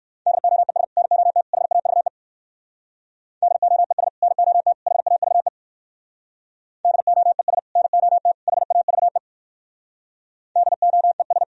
Morse code, heard as a keyed tone, about 60 wpm, 700 Hz. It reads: DOES NOT HAVE DOES NOT HAVE DOES NOT HAVE DOES